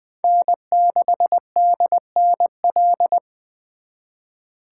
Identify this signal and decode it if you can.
Morse code, heard as a keyed tone, about 25 wpm, 700 Hz. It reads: N6DNL
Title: Speed Racing - USA 1x3 Call Signs 20wpm